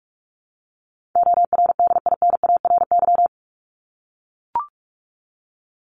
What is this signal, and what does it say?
Morse code, heard as a keyed tone, about 44 wpm, 700 Hz. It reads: ORDINARY
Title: Top 1751-2000 Words 45wpm